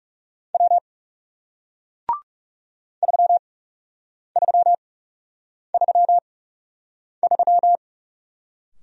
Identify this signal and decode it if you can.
Morse code, heard as a keyed tone, about 40 wpm, spaced as for 8 wpm, 700 Hz. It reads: W 3333